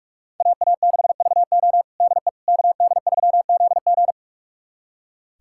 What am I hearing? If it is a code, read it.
Morse code, heard as a keyed tone, about 42 wpm, 700 Hz. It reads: AA6VO DE KD3ZG
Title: USA Callsign DE Callsign - 5 Characters - 45wpm